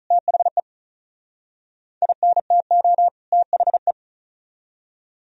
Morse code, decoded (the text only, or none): THE INTO THE